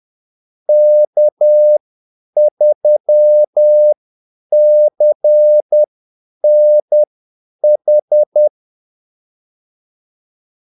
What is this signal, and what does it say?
Morse code, heard as a keyed tone, about 10 wpm, spaced as for 8 wpm, 600 Hz. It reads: K3CNH
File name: Callsigns-10-8wpm